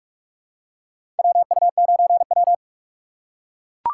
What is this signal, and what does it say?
Morse code, heard as a keyed tone, about 43 wpm, 700 Hz. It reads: WU9W
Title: USA Call Signs - 4 Characters - 2x Character Spacing - 45wpm